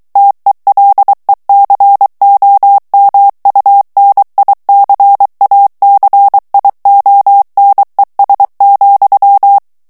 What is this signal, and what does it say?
Morse code, heard as a keyed tone, about 23 wpm, 800 Hz. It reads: TELECOMUNICACIONES,
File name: ESPAGNOL TEXTE EN CLAIR 2 VITESSE 22 MOTS MINUTES